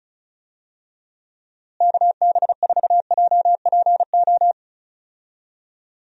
Morse code, heard as a keyed tone, about 36 wpm, 700 Hz. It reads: KB4JPO
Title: USA Call Signs - 6 Characters - Repeated 3x - 35wpm